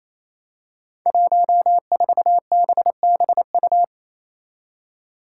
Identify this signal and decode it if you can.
Morse code, heard as a keyed tone, about 28 wpm, 700 Hz. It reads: 14BBU
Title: Five Challenging Letter-Number Combinations 28wpm